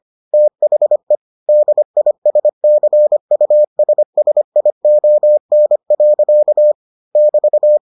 Morse code, heard as a keyed tone, about 25 wpm, 600 Hz. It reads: THE DISCUSSION. <BT>